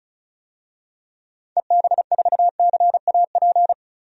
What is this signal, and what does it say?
Morse code, heard as a keyed tone, about 36 wpm, 700 Hz. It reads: EB4CAP
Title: DXCC-100 Call Signs - 6 Characters - 35wpm